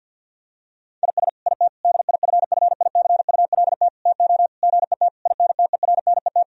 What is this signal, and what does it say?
Morse code, heard as a keyed tone, about 50 wpm, 700 Hz. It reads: IS IT DIFFICULT TO GET INTERNET
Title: Sentences from Top 1000 Words - Review Part 4 - 50wpm